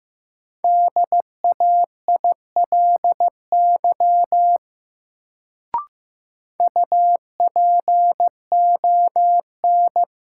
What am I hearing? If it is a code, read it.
Morse code, heard as a keyed tone, about 15 wpm, 700 Hz. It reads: DAILY UPON
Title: Sentences from Top 1100 Words - New Word Drill Alt - 15wpm